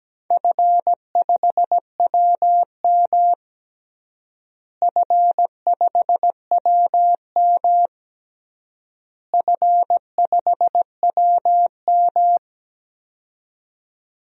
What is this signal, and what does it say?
Morse code, heard as a keyed tone, about 17 wpm, 700 Hz. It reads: F5WM F5WM F5WM
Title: DXCC-100 Call Signs - 4 Characters - Repeated 3x - 17wpm